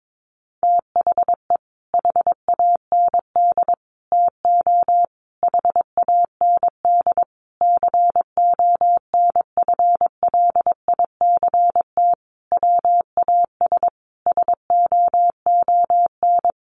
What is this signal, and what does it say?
Morse code, heard as a keyed tone, about 22 wpm, 700 Hz. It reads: THE HAND TO HAND CONFLICT WAS SOON
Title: Sentences from Top 1300 Words 22wpm